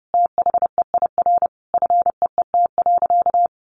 Morse code, heard as a keyed tone, about 30 wpm, 700 Hz. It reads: THEIR FEET.
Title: Sentences from Top 1000 Words 30wpm